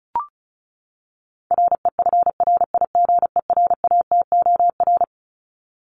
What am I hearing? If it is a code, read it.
Morse code, heard as a keyed tone, about 34 wpm, 700 Hz. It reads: REFRIGERATOR
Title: Top 1501-1750 Words 35wpm